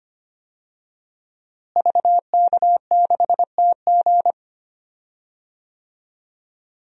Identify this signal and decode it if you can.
Morse code, heard as a keyed tone, about 25 wpm, 700 Hz. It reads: VK6TG
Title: DXCC-100 Call Signs - 5 Characters - Repeated 3x - 25wpm